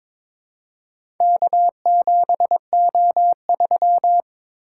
Morse code, heard as a keyed tone, about 22 wpm, 700 Hz. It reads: K7O3
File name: practice / 22wpm / Four Letter-Number Combinations - 22wpm